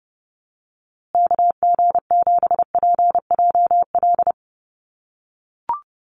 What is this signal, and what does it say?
Morse code, heard as a keyed tone, about 30 wpm, 700 Hz. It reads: KG7PJL
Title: USA Call Signs Repeated 3x 30wpm